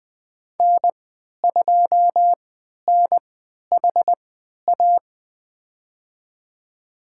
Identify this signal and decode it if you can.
Morse code, heard as a keyed tone, about 20 wpm, spaced as for 11 wpm, 700 Hz. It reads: N2NHA
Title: USA Call Signs - 3x Character Spacing - 20wpm